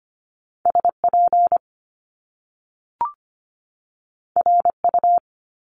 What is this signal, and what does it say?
Morse code, heard as a keyed tone, about 25 wpm, 700 Hz. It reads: SP RU
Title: Top Two Letter Combinations in Words 25wpm